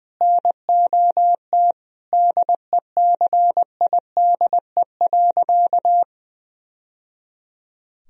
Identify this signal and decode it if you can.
Morse code, heard as a keyed tone, about 20 wpm, 700 Hz. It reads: NOT DECIDE.
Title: Sentences from Top 1000 Words - Review Part 3 - 20wpm